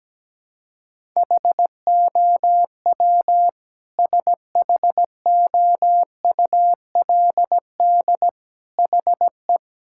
Morse code, heard as a keyed tone, about 17 wpm, 700 Hz. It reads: HOW SHOULD HE
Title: Sentences from Top 1000 Words - Review Part 6 - 17wpm